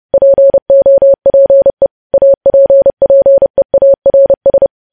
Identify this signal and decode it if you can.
Morse code, heard as a keyed tone, about 30 wpm, 550 Hz. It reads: POPE APPEARS
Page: news headlines in morse code at 30 wpm